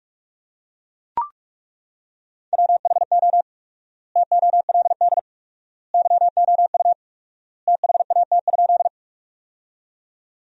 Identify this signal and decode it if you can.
Morse code, heard as a keyed tone, about 41 wpm, 700 Hz. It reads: WHO TOLD YOU THAT?
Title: 4x Word-Spacing - Sentences from Top 400 Words - 45wpm